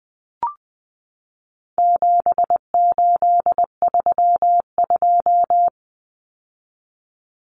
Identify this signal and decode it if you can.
Morse code, heard as a keyed tone, about 20 wpm, 700 Hz. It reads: 7832
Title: Common Cut Numbers - 4-Digit 20wpm